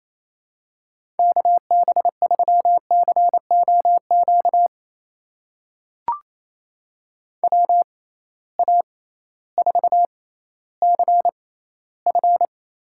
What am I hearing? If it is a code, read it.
Morse code, heard as a keyed tone, about 28 wpm, spaced as for 15 wpm, 700 Hz. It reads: KB3COQ WA4CF
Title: USA Call Signs - 6 Characters - 6x Character Spacing - 28wpm